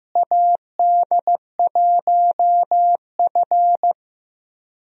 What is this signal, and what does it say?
Morse code, heard as a keyed tone, about 15 wpm, 700 Hz. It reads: AD1F
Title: USA Call Signs - 4 Characters - Repeated 3x - 15wpm